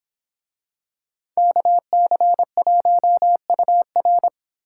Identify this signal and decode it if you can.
Morse code, heard as a keyed tone, about 26 wpm, 700 Hz. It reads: KC1UR
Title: Speed Racing - USA Call Signs 22wpm